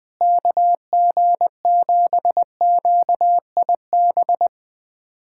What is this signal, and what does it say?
Morse code, heard as a keyed tone, about 20 wpm, 700 Hz. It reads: KG7QIB
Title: USA Callsign DE Callsign - 6 Characters - 20wpm